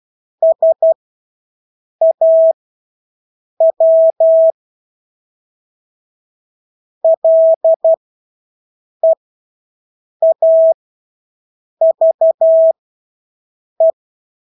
Morse code, heard as a keyed tone, about 12 wpm, spaced as for 6 wpm, 650 Hz. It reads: SAW LEAVE
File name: Engelske ord 12-6 wpm